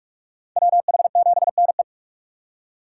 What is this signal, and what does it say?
Morse code, heard as a keyed tone, about 42 wpm, 700 Hz. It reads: WH7NE
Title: USA Call Signs - 5 Characters - 6x Character Spacing - 45wpm